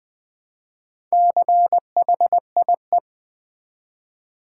Morse code, heard as a keyed tone, about 20 wpm, 700 Hz. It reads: CHIE
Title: Build A Word - 1 Letter At A Time - 3 to 5-Letter Words - 20wpm